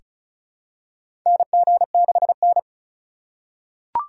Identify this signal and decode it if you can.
Morse code, heard as a keyed tone, about 34 wpm, 700 Hz. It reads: NG6N
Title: USA Call Signs - 4 Characters - 3x Character Spacing - 35wpm